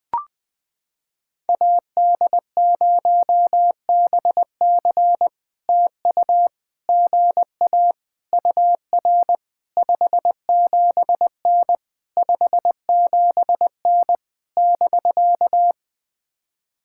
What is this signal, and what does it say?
Morse code, heard as a keyed tone, about 21 wpm, 700 Hz. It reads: AD0BC TU GA UR 57N 57N <BK>
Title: POTA QSO Practice - 20wpm